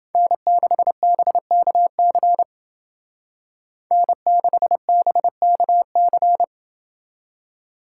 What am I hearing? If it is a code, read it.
Morse code, heard as a keyed tone, about 29 wpm, 700 Hz. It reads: N6BKC N6BKC